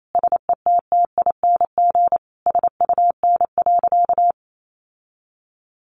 Morse code, heard as a keyed tone, about 28 wpm, 700 Hz. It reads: SETTING SUN.
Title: Sentences from Top 1300 Words 28wpm